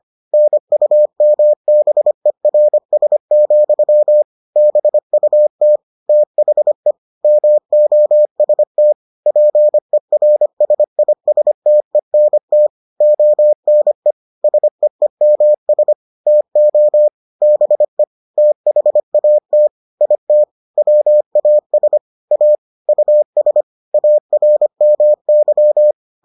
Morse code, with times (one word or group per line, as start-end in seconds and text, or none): 0.3-4.2 NUMBERS,
4.6-5.8 BUT
6.1-6.9 THE
7.2-8.9 MOST
9.3-12.7 PERSISTENT
13.0-14.1 ONE
14.4-15.9 SEEMS
16.3-17.1 TO
17.4-18.0 BE
18.4-19.7 THAT
20.0-20.4 IT
20.8-22.0 WAS
22.3-22.6 A
22.9-23.6 US
23.9-25.9 ARMY